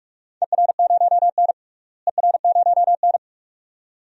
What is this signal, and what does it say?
Morse code, heard as a keyed tone, about 42 wpm, 700 Hz. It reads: ER0N ER0N
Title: DXCC-100 Call Signs - 4 Characters - Repeated 2x - 45wpm